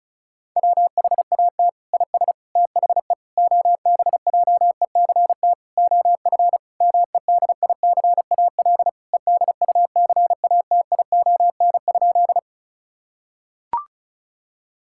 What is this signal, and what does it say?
Morse code, heard as a keyed tone, about 34 wpm, 700 Hz. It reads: WHAT IS THE OBJECT OF MEDICAL EDUCATION?
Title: Sentences from Top 1000 Words - Review Part 6 - 35wpm